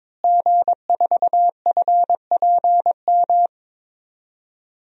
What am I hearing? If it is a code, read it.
Morse code, heard as a keyed tone, about 22 wpm, 700 Hz. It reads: G4FPM
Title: DXCC-100 Call Signs - 5 Characters - Repeated 2x - 22wpm